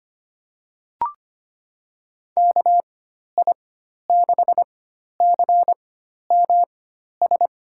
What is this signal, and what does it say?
Morse code, heard as a keyed tone, about 25 wpm, spaced as for 12 wpm, 700 Hz. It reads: KI6CMS
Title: USA Call Signs - 6 Characters - 4x Character Spacing - 25wpm